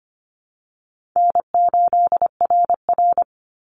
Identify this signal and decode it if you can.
Morse code, heard as a keyed tone, about 25 wpm, 700 Hz. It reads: N8RR
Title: Call Signs - 25wpm